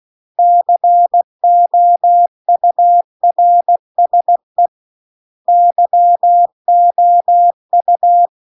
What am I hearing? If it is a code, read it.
Morse code, heard as a keyed tone, about 16 wpm, 700 Hz. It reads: COURSE YOU